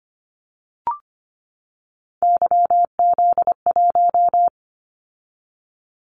Morse code, heard as a keyed tone, about 25 wpm, 700 Hz. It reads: YZ1
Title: Callsign Prefixes for Top 25 DXCC - 25wpm